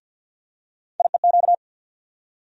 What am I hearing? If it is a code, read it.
Morse code, heard as a keyed tone, about 49 wpm, 700 Hz. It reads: IEQ
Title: USA Call Sign Postfixes - 3 Letters 50wpm